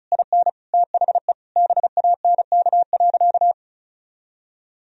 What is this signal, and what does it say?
Morse code, heard as a keyed tone, about 34 wpm, 700 Hz. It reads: IN THE BANK.
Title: Sentences from Top 1000 Words - Review Part 4 - 35wpm